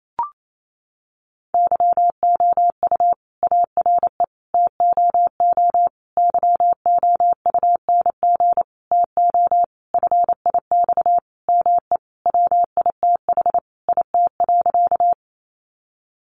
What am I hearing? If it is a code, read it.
Morse code, heard as a keyed tone, about 28 wpm, 700 Hz. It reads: YOU ARE TOO YOUNG TO FIX ME WITH IT.
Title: Sentences from Top 1400 Words 28wpm